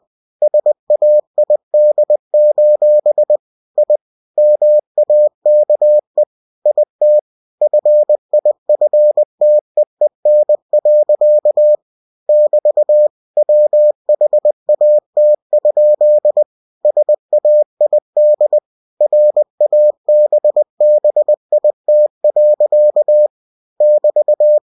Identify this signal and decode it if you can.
Morse code, heard as a keyed tone, about 20 wpm, 600 Hz. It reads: SAID: I MAKE IT FIFTEEN. <BT> WHAT? SAID RABBIT. <BT>
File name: cw-books / winnie-the-pooh / winnie-20wpm-page-0060